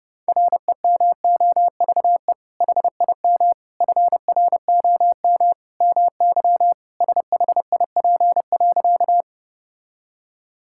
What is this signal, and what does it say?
Morse code, heard as a keyed tone, about 30 wpm, 700 Hz. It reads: REMOVE HIM FROM MY SHIP.
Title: Sentences from Top 1000 Words - Review Part 5 - 30wpm